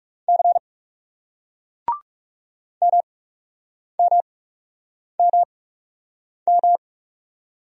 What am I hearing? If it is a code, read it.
Morse code, heard as a keyed tone, about 38 wpm, spaced as for 8 wpm, 700 Hz. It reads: C MMMM